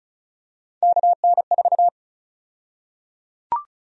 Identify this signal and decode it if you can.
Morse code, heard as a keyed tone, about 34 wpm, 700 Hz. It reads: KN4
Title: USA Call Sign Prefixes - 35wpm